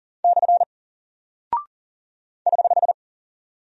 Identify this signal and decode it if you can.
Morse code, heard as a keyed tone, about 40 wpm, 700 Hz. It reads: / <HH>
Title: Single Character (Letter:Number:Punctuation:Prosign) Alt 40wpm